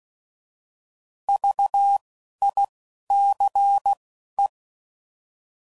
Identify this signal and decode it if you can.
Morse code, heard as a keyed tone, about 16 wpm, 800 Hz. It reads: V I C E
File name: Prefixes / By First Letter / Vs